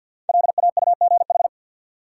Morse code, heard as a keyed tone, about 46 wpm, 700 Hz. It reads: LAUGH
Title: Sentences from Top 500 Words - New Word Drill - 50wpm